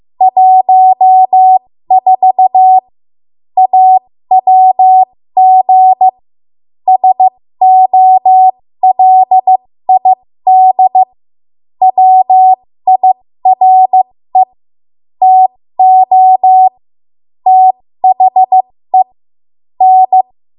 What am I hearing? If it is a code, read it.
Morse code, heard as a keyed tone, about 15 wpm, spaced as for 13 wpm, 750 Hz. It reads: 14 AWG SOLID WIRE TO THE N
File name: Morse / 13WPM